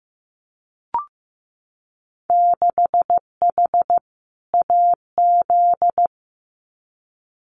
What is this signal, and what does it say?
Morse code, heard as a keyed tone, about 15 wpm, 700 Hz. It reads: 6H AZ
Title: Winter Field Day Exchanges Encore 15wpm